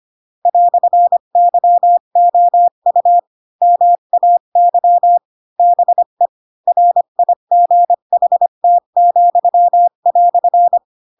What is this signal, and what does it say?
Morse code, heard as a keyed tone, about 25 wpm, 700 Hz. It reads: "YOU MAY BE RIGHT,"
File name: Pinocchio / Pinocchio ch7 20 25 ik1hge